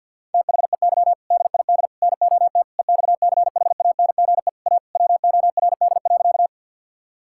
Morse code, heard as a keyed tone, about 50 wpm, 700 Hz. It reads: THEY DID NOT EXCHANGE A WORD.
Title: Sentences from Top 1000 Words - Review Part 6 - 50wpm